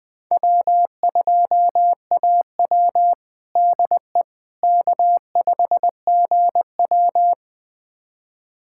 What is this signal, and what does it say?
Morse code, heard as a keyed tone, about 20 wpm, 700 Hz. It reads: W2AW DE K5GW
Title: USA Callsign DE Callsign - 4 Characters - 20wpm